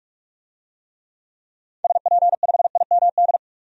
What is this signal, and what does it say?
Morse code, heard as a keyed tone, about 43 wpm, 700 Hz. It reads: SP5IMD
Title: DXCC-100 Call Signs - 6 Characters - 45wpm